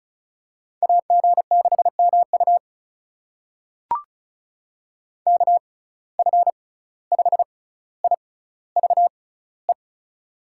Morse code, heard as a keyed tone, about 36 wpm, spaced as for 17 wpm, 700 Hz. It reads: AG6MU KF5IVE
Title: USA Call Signs - 6x Character Spacing - 35wpm